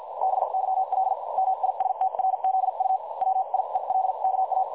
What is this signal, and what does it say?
Morse code, heard as a keyed tone, about 50 wpm, spaced as for 43 wpm, 760 Hz. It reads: BYL MISTRZU WIELKA
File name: dlazaawansowanych / nagraniazpasma / websdr recording 3565.0kHz